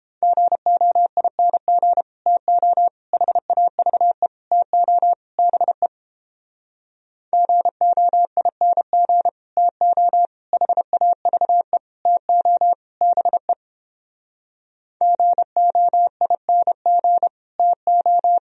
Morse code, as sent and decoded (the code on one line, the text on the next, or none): --. --- .. -. --. / - --- / .... .- ...- . / - --- / -... / --. --- .. -. --. / - --- / .... .- ...- . / - --- / -... / --. --- .. -. --. / - ---
GOING TO HAVE TO BE GOING TO HAVE TO BE GOING TO